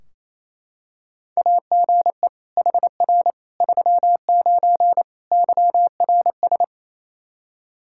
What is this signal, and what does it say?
Morse code, heard as a keyed tone, about 28 wpm, 700 Hz. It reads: AGE HR 39 YRS
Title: Practice QSO - Exchange 3 - More Info - 28wpm